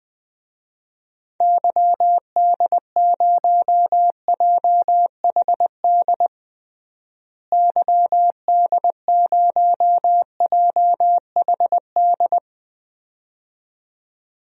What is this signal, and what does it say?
Morse code, heard as a keyed tone, about 20 wpm, 700 Hz. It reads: YD0JHD YD0JHD
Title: DXCC-100 Call Signs - 6 Characters - Repeated 2x - 20wpm